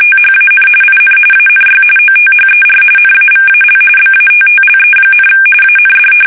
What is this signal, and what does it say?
RTTY, about 45.45 baud. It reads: THAN IN ANY OTHER NAVAL BATTLE, AND
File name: fsk850 / battleshipnj